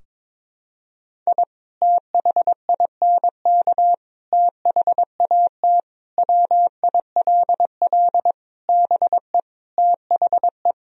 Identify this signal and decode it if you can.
Morse code, heard as a keyed tone, about 22 wpm, 700 Hz. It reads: I THINK THAT WILL BE THE
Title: Sentences from Top 1000 Words - Review Part 2 - 22wpm